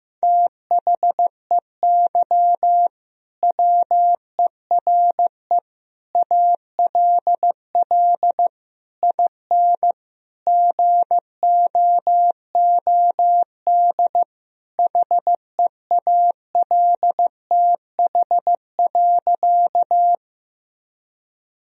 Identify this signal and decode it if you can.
Morse code, heard as a keyed tone, about 15 wpm, 700 Hz. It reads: THEY WERE ALL IN GOOD HEALTH.
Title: Sentences from Top 1000 Words - Review Part 3 - 15wpm